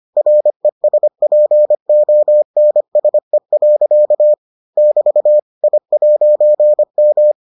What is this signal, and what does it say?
Morse code, heard as a keyed tone, about 25 wpm, 600 Hz. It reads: RESPONSE. <BT> I'M